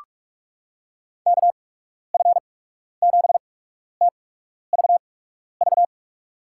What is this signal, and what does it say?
Morse code, heard as a keyed tone, about 47 wpm, spaced as for 12 wpm, 700 Hz. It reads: KF7TVV
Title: USA Call Signs - 6 Characters - 8x Character Spacing - 45wpm